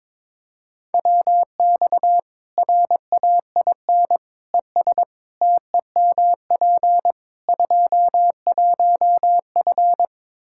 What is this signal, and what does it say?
Morse code, heard as a keyed tone, about 22 wpm, 700 Hz. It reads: WX RAIN ES TEMP 21F
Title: Practice QSO - Exchange 2 - Info - 22wpm